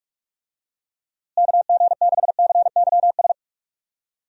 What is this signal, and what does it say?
Morse code, heard as a keyed tone, about 46 wpm, 700 Hz. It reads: KG6CYS
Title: USA Call Signs - 6 Characters - Repeated 2x - 45wpm